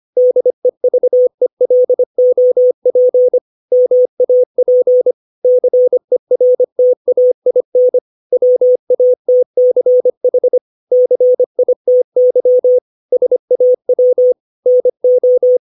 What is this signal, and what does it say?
Morse code, heard as a keyed tone, about 25 wpm, 500 Hz. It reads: DEVELOP MAP CERTAIN WATCH CITY SAW NO
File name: 632 words / 632 Words 25WPM